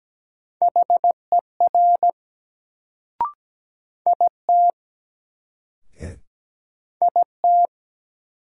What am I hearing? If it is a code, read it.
Morse code, heard as a keyed tone, about 17 wpm, 700 Hz. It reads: HER IT IT
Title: Sets of 2 Words - New Word Drill - 17wpm